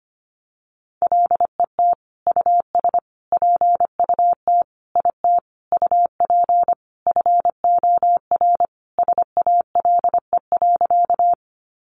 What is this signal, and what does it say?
Morse code, heard as a keyed tone, about 25 wpm, 700 Hz. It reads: LET US PUT IT UP FOR SALE.
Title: Sentences from Top 1400 Words 25wpm